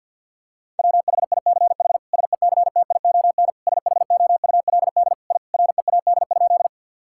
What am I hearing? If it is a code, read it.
Morse code, heard as a keyed tone, about 50 wpm, 700 Hz. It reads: WHICH SECTION SHOULD I READ?
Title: Sentences from Top 1000 Words - Review Part 5 - 50wpm